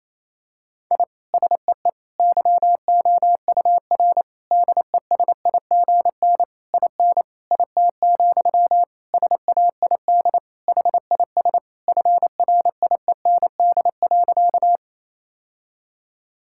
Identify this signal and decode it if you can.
Morse code, heard as a keyed tone, about 28 wpm, 700 Hz. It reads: I SEE YOUR DESIGN IN IT, SAID HIS FRIEND.
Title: Sentences from Top 1000 Words - Review Part 5 - 28wpm